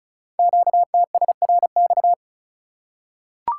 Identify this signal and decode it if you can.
Morse code, heard as a keyed tone, about 34 wpm, 700 Hz. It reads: QTSRX
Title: Code Groups - 5 Letters - 35wpm